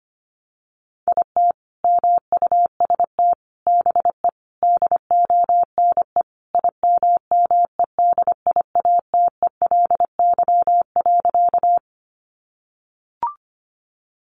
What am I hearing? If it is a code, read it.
Morse code, heard as a keyed tone, about 25 wpm, 700 Hz. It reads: IT MUST BE DONE IMMEDIATELY.
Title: Sentences from Top 1100 Words 25wpm